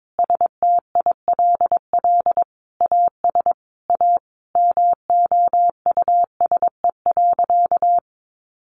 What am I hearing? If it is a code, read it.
Morse code, heard as a keyed tone, about 22 wpm, 700 Hz. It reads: STILL AS A MOUSE.
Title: Sentences from Top 700 Words 22wpm